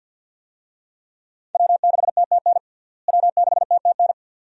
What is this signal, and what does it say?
Morse code, heard as a keyed tone, about 53 wpm, 700 Hz. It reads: W6TTN W6TTN
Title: USA Call Signs - 5 Characters - Repeated 2x - 50wpm